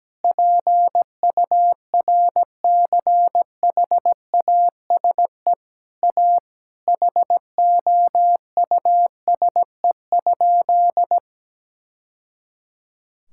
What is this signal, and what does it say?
Morse code, heard as a keyed tone, about 17 wpm, 700 Hz. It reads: PURCHASE A HOUSE?
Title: Sentences from Top 1000 Words - Review Part 6 - 17wpm